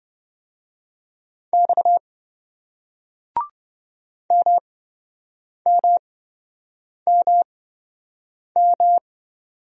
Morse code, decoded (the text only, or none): X MMMM